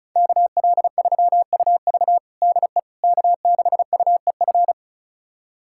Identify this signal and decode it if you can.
Morse code, heard as a keyed tone, about 36 wpm, 700 Hz. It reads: KL3UV DE K6UEF
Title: USA Callsign DE Callsign - 5 Characters - 35wpm